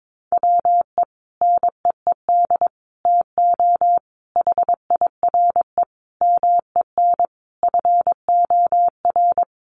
Morse code, heard as a keyed tone, about 22 wpm, 700 Hz. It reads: WE NEED TO HIRE MEN FOR
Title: Sentences from Top 1400 Words 22wpm